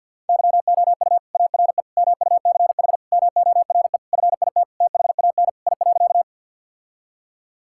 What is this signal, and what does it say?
Morse code, heard as a keyed tone, about 47 wpm, 700 Hz. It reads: YOU ARE MUCH MORE FIT THAN I.
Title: Sentences from Top 1000 Words - Review Part 5 - 50wpm